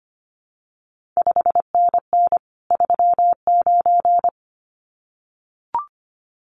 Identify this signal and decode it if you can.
Morse code, heard as a keyed tone, about 25 wpm, 700 Hz. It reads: 5NN 39
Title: CQ WW DX Contest Exchange Encore - 25wpm